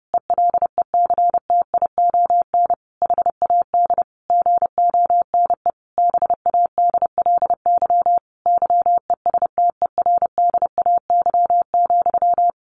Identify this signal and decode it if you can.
Morse code, heard as a keyed tone, about 30 wpm, 700 Hz. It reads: ELECTION HAD GONE BADLY YESTERDAY,
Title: A Sound Of Thunder 30wpm